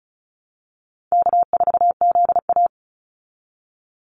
Morse code, heard as a keyed tone, about 34 wpm, 700 Hz. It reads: K4ZA
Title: Easy Call Signs Repeated 2x - 35wpm